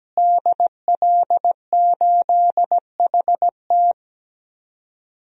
Morse code, heard as a keyed tone, about 17 wpm, 700 Hz. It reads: DL8HT
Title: DXCC-100 Call Signs - 5 Characters - Repeated 3x - 17wpm